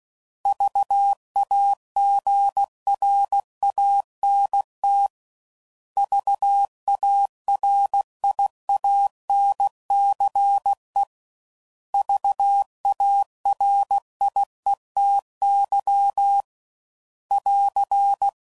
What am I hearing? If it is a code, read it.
Morse code, heard as a keyed tone, about 16 wpm, 800 Hz. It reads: VAGRANT VARIANCE VARIETY <AR>
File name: Prefixes / By First Letter / Vs